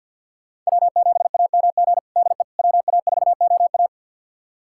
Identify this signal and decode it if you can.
Morse code, heard as a keyed tone, about 51 wpm, 700 Hz. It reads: W7AMG DE WA4OA
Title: USA Callsign DE Callsign - 5 Characters - 50wpm